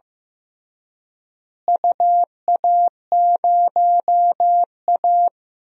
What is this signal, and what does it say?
Morse code, heard as a keyed tone, about 16 wpm, 700 Hz. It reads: UA0A
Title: DXCC-100 Call Signs - 4 Characters - Repeated 3x - 15wpm